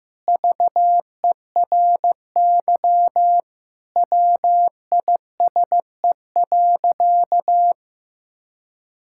Morse code, heard as a keyed tone, about 16 wpm, 700 Hz. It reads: VERY WISE.
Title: Sentences from Top 1600 Words 15wpm